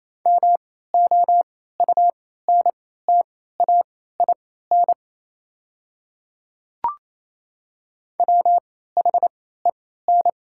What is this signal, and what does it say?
Morse code, heard as a keyed tone, about 28 wpm, spaced as for 16 wpm, 700 Hz. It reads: MOUNTAIN WHEN
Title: 3x Character Spacing - Top 300 Words - 28wpm